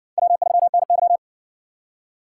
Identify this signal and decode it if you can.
Morse code, heard as a keyed tone, about 60 wpm, 700 Hz. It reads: W3NY